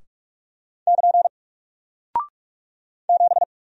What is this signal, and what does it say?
Morse code, heard as a keyed tone, about 44 wpm, 700 Hz. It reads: <KN> 7
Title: Single Character (Letter:Number:Punctuation:Prosign) Alt 45wpm